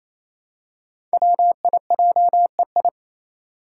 Morse code, heard as a keyed tone, about 27 wpm, 700 Hz. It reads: WIJEI
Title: Five Challenging Letter-Number Combinations 28wpm